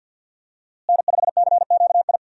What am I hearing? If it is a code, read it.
Morse code, heard as a keyed tone, about 52 wpm, 700 Hz. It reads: N5CQI
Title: USA Call Signs - 5 Characters - Repeated 3x - 50wpm